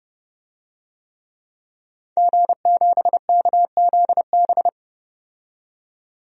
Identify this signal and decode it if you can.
Morse code, heard as a keyed tone, about 32 wpm, 700 Hz. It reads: G7KZB
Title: DXCC-100 Call Signs - 5 Characters - 30wpm